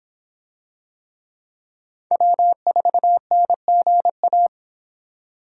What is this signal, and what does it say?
Morse code, heard as a keyed tone, about 25 wpm, 700 Hz. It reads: W4NGA